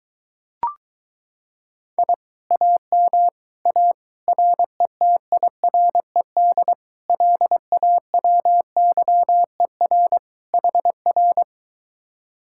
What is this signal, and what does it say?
Morse code, heard as a keyed tone, about 23 wpm, 700 Hz. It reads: I AM A RETIRED LAWYER HR